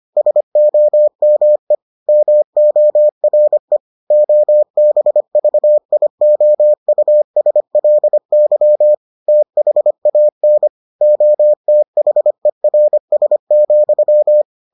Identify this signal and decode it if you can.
Morse code, heard as a keyed tone, about 25 wpm, 600 Hz. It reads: SOME MORE OBVIOUSLY THAN OTHERS,